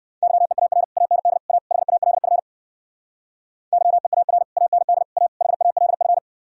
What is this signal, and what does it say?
Morse code, heard as a keyed tone, about 66 wpm, 700 Hz. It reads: YEAR AND A HALF YEAR AND A HALF